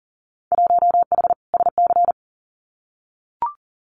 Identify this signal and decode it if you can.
Morse code, heard as a keyed tone, about 40 wpm, 700 Hz. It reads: 1H SC
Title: Winter Field Day Exchanges 40wpm